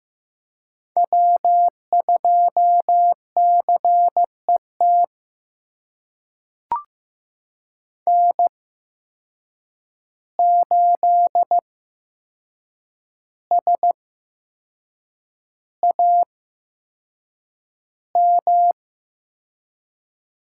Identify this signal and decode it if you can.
Morse code, heard as a keyed tone, about 15 wpm, spaced as for 6 wpm, 700 Hz. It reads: W2CET N8SAM
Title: USA Call Signs - 5 Characters - 8x Character Spacing - 15wpm